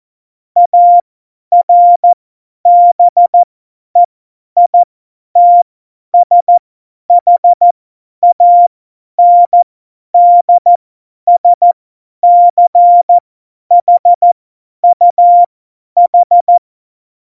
Morse code, heard as a keyed tone, about 14 wpm, spaced as for 10 wpm, 700 Hz. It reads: ARBEITSHANDSCHUH